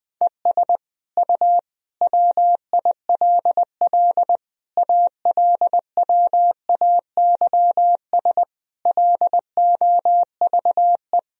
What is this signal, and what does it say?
Morse code, heard as a keyed tone, about 21 wpm, 700 Hz. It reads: ES U WILL ALWAYS LOVE
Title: Sentences from Top 400 Words and Abbreviations 20wpm